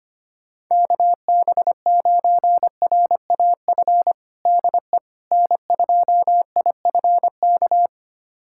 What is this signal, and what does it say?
Morse code, heard as a keyed tone, about 25 wpm, 700 Hz. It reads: KB9RAF DE N2IFK
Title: USA Callsign DE Callsign 25wpm